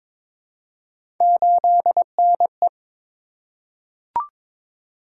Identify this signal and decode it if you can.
Morse code, heard as a keyed tone, about 22 wpm, 700 Hz. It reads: 8NE
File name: practice / 22wpm / Three Letter-Number Combinations - 22wpm